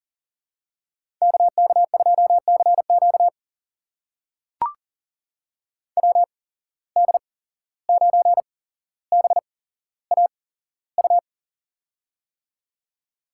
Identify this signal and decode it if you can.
Morse code, heard as a keyed tone, about 38 wpm, spaced as for 16 wpm, 700 Hz. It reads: KK2CQ WD9BAU
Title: USA Call Signs - 8x Character Spacing - 40wpm